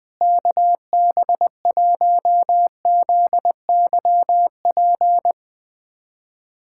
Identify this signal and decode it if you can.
Morse code, heard as a keyed tone, about 21 wpm, 700 Hz. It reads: KB1ZYP
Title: USA Callsign DE Callsign 20wpm